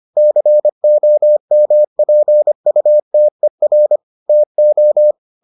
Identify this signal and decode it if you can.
Morse code, heard as a keyed tone, about 25 wpm, 600 Hz. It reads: COMPUTER TO